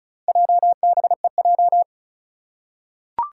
Morse code, heard as a keyed tone, about 34 wpm, 700 Hz. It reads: JBEJ
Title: Four Challenging Letter-Number Combinations 35wpm